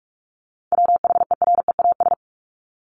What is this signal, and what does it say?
Morse code, heard as a keyed tone, about 44 wpm, 700 Hz. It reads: WHEREAS
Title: Top 1501-1750 Words 45wpm